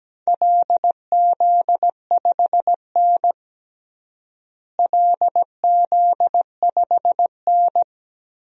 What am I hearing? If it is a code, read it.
Morse code, heard as a keyed tone, about 17 wpm, 700 Hz. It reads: LZ5N LZ5N
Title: DXCC-100 Call Signs - 4 Characters - Repeated 2x - 17wpm